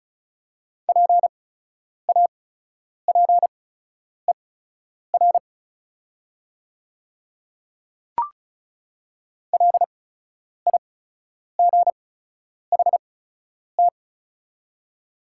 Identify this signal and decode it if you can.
Morse code, heard as a keyed tone, about 36 wpm, spaced as for 10 wpm, 700 Hz. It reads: PAPER LIGHT
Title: 8x Character Spacing - Top 300 Words - 35wpm